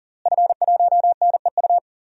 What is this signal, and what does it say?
Morse code, heard as a keyed tone, about 37 wpm, 700 Hz. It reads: F1NEU
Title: DXCC-100 Call Signs - 5 Characters - Repeated 2x - 40wpm